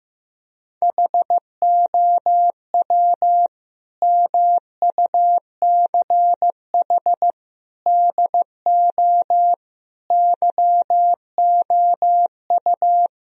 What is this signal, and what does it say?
Morse code, heard as a keyed tone, about 16 wpm, 700 Hz. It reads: HOW MUCH DO YOU
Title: Sentences from Top 1000 Words - Review Part 2 - 15wpm